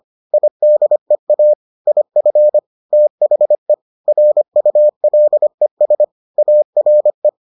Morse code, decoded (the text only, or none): IDEA IF THE RULES ARE